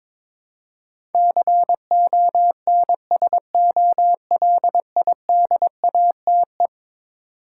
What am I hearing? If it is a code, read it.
Morse code, heard as a keyed tone, about 21 wpm, 700 Hz. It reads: CONSOLIDATE
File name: practice / 22wpm / Eleven Letter Words - 22wpm